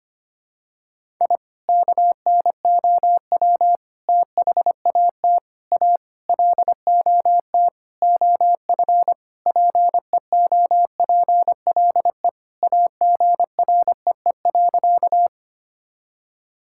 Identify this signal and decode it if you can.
Morse code, heard as a keyed tone, about 25 wpm, 700 Hz. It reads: I KNOW THAT A LOT OF PEOPLE AGREE.
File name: practice / 25wpm / Sentences with 4-Word Phrases - 25wpm